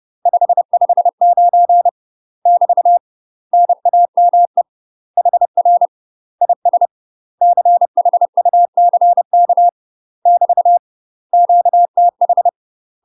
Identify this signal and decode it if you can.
Morse code, heard as a keyed tone, about 30 wpm, 700 Hz. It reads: 559 <BT> NAME HR IS CHUCK <BT> QTH
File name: UK MASTER QSOs TRAINING / 30WPM / UK MASTER QSOs 0043